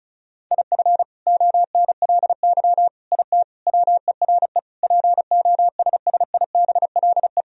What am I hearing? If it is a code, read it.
Morse code, heard as a keyed tone, about 34 wpm, 700 Hz. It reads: IF ONLY IT WERE POSSIBLE
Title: Sentences from Top 500 Words and Abbreviations 35wpm